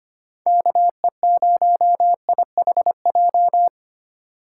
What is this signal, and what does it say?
Morse code, heard as a keyed tone, about 25 wpm, 700 Hz. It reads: KE0IHJ
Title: USA Call Signs - 6 Characters - 8x Character Spacing - 25wpm